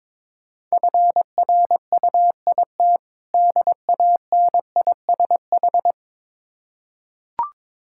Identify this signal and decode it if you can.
Morse code, heard as a keyed tone, about 22 wpm, 690 Hz. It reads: FRUIT DANISH
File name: Difficult Words with Hints - 22wpm